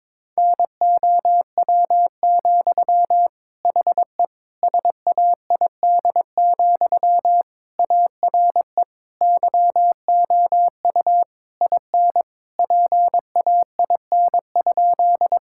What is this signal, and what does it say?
Morse code, heard as a keyed tone, about 22 wpm, 700 Hz. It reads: NOW, HE SAID, ARE YOU IN PAIN?
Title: Sentences from Top 1600 Words 22wpm